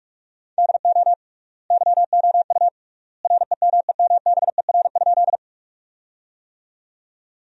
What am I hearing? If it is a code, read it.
Morse code, heard as a keyed tone, about 45 wpm, 700 Hz. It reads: DO YOU REMEMBER?
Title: 3x Word-Spacing - Sentences from Top 400 Words - 45wpm